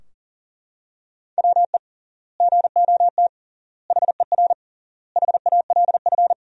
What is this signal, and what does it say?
Morse code, heard as a keyed tone, about 40 wpm, 700 Hz. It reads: WE GOT HER HALF
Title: 3x Word-Spacing - Sentences from Top 400 Words - 40wpm